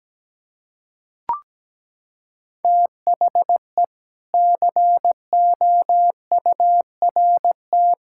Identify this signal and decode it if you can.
Morse code, heard as a keyed tone, about 17 wpm, 700 Hz. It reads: THE COURT
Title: Sentences from Top 1600 Words 17wpm